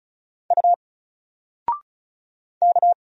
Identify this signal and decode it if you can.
Morse code, heard as a keyed tone, about 35 wpm, 700 Hz. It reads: U K
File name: Single-Letter-Number-35wpm